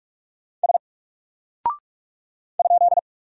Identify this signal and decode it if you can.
Morse code, heard as a keyed tone, about 46 wpm, 700 Hz. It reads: S ?
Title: Single Character (Letter:Number:Punctuation:Prosign) 45wpm